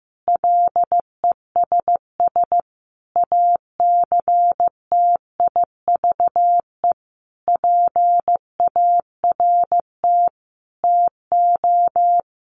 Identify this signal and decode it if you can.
Morse code, heard as a keyed tone, about 15 wpm, 700 Hz. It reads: LESS ACTIVE PART TO
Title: Sentences from Top 900 Words 15wpm